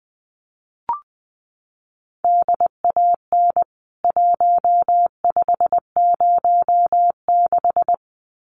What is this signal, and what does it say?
Morse code, heard as a keyed tone, about 19 wpm, 700 Hz. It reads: DAN 1506
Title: CWT Exchange 20wpm